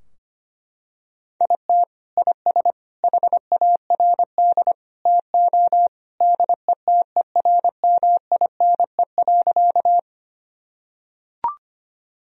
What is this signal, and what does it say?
Morse code, heard as a keyed tone, about 25 wpm, 700 Hz. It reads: IT IS HARD TO DETERMINE.
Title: Sentences from Top 1000 Words - Review Part 4 - 25wpm